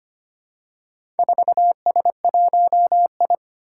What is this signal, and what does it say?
Morse code, heard as a keyed tone, about 25 wpm, 700 Hz. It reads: 4S1I
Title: Four Challenging Letter-Number Combinations 25wpm